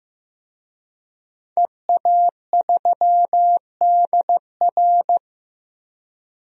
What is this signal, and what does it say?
Morse code, heard as a keyed tone, about 15 wpm, 700 Hz. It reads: EA3DR
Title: DXCC-100 Call Signs - 5 Characters - 15wpm